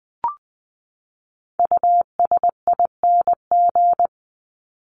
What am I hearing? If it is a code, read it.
Morse code, heard as a keyed tone, about 20 wpm, 700 Hz. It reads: USING
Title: ING Suffix - Top 100 Words with Base Words 20wpm